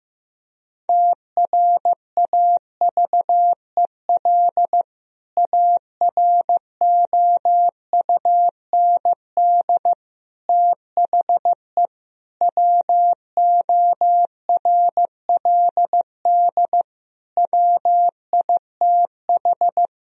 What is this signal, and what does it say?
Morse code, heard as a keyed tone, about 15 wpm, 700 Hz. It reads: TRAVEL AROUND THE WORLD WITH
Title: Sentences from Top 1000 Words - Review Part 3 - 15wpm